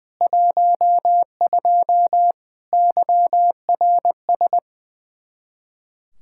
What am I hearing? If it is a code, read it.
Morse code, heard as a keyed tone, about 20 wpm, 700 Hz. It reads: 12 YRS
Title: Practice QSO - Exchange 3 - More Info - 20wpm